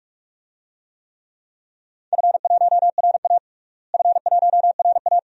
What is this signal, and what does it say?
Morse code, heard as a keyed tone, about 41 wpm, 700 Hz. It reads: F1RA F1RA
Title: DXCC-100 Call Signs - 4 Characters - Repeated 2x - 45wpm